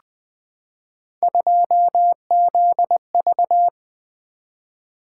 Morse code, heard as a keyed tone, about 20 wpm, 700 Hz. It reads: 2ZV
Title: Three Letter-Number Combinations - 20wpm